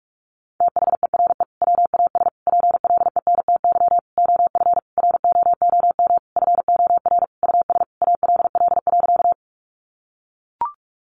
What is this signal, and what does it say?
Morse code, heard as a keyed tone, about 44 wpm, 700 Hz. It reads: THERE WAS PLENTY OF ROOM FOR US ALL.
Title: Sentences from Top 1300 Words 45wpm